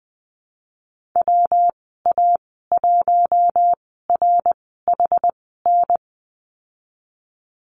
Morse code, heard as a keyed tone, about 20 wpm, spaced as for 14 wpm, 700 Hz. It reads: WA1RHN
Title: 122 Lesson 20g - Number 3 - Call Signs 20wpm